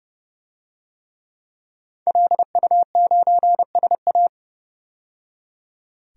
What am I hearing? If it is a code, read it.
Morse code, heard as a keyed tone, about 30 wpm, 700 Hz. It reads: LU9SA
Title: DXCC-100 Call Signs - 5 Characters - 30wpm